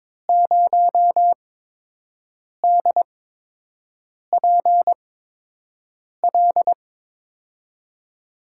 Morse code, heard as a keyed tone, about 21 wpm, spaced as for 6 wpm, 700 Hz. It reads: 0DPL
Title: USA Call Signs - 5 Characters - 8x Character Spacing - 22wpm